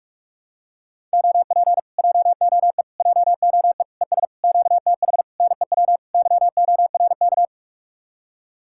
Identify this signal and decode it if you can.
Morse code, heard as a keyed tone, about 44 wpm, 700 Hz. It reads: OP JOE JOE ES QTH NEW YORK